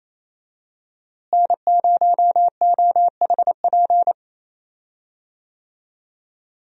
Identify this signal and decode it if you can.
Morse code, heard as a keyed tone, about 28 wpm, 700 Hz. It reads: N0OHP
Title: USA Call Signs - 5 Characters - Repeated 3x - 28wpm